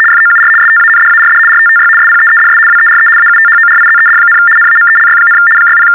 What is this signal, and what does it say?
RTTY, about 75 baud. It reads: FOUR STANDARD RTTY SPEEDS WERE AUTHORIZED, NAMELY, 60
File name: teletype-RTTY-75